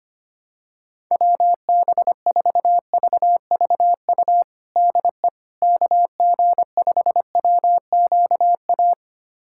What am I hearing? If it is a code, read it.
Morse code, heard as a keyed tone, about 25 wpm, 700 Hz. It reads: WB4VVU DE KG5WQA
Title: USA Callsign DE Callsign - 6 Characters - 25wpm